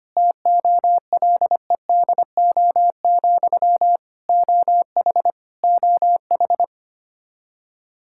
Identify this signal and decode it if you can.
Morse code, heard as a keyed tone, about 25 wpm, 700 Hz. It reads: TOLEDO, OH OH
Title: Practice QSO - Exchange 1 - Beginning - 25wpm